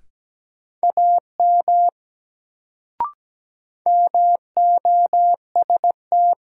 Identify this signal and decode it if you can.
Morse code, heard as a keyed tone, about 17 wpm, 700 Hz. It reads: AM MOST
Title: Sets of 2 Words - New Word Drill - 17wpm